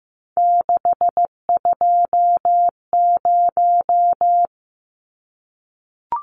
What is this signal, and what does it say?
Morse code, heard as a keyed tone, about 15 wpm, 700 Hz. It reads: 620
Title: Numbers - 3-Digit 15wpm